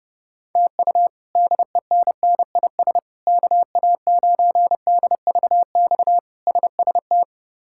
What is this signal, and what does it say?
Morse code, heard as a keyed tone, about 32 wpm, 700 Hz. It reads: TU DENNIS KA9DVX SST
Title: K1USN SST QSO Practice - 30wpm